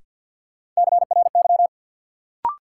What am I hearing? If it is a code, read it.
Morse code, heard as a keyed tone, about 53 wpm, 700 Hz. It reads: CRY
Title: Sets of 5 Words Encore - New Word Drill - 50wpm